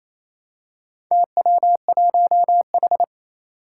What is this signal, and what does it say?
Morse code, heard as a keyed tone, about 27 wpm, 700 Hz. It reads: TW1H
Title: Four Challenging Letter-Number Combinations 28wpm